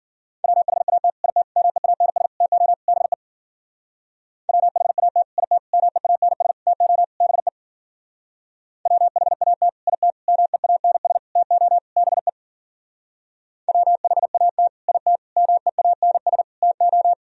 Morse code, heard as a keyed tone, about 52 wpm, 700 Hz. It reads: WHAT IT MEANS TO BE WHAT IT MEANS TO BE WHAT IT MEANS TO BE WHAT IT MEANS TO